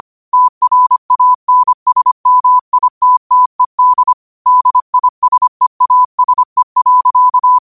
Morse code, heard as a keyed tone, about 25 wpm, 1000 Hz. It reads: TRANSMITTED DISEASE.
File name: qotdfiles / 25wpm